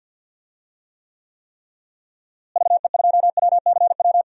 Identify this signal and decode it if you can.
Morse code, heard as a keyed tone, about 46 wpm, 700 Hz. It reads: VE2WCW
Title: DXCC-100 Call Signs - 6 Characters - Repeated 3x - 50wpm